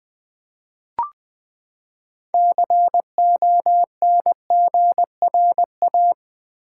Morse code, heard as a keyed tone, about 20 wpm, 700 Hz. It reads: CONGRA
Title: Build A Word - 6 Letters At A Time - 12-Letter Words - 20wpm